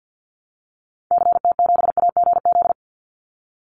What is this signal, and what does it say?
Morse code, heard as a keyed tone, about 50 wpm, 700 Hz. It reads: CT7AGZ
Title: Call Signs - 50wpm